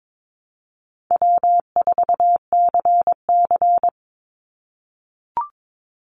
Morse code, heard as a keyed tone, about 22 wpm, 700 Hz. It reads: W4CC
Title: Easy Call Signs - 22wpm